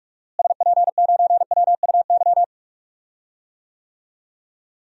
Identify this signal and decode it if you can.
Morse code, heard as a keyed tone, about 45 wpm, 700 Hz. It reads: SP9WUY
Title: DXCC-100 Call Signs - 6 Characters - Repeated 2x - 45wpm